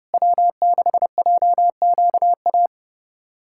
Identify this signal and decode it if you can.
Morse code, heard as a keyed tone, about 30 wpm, 700 Hz. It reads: W6JQA
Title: SOTA QSO Practice 30wpm